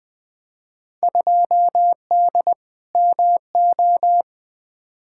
Morse code, heard as a keyed tone, about 21 wpm, 700 Hz. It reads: 2D MO
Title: Field Day Exchanges - 20wpm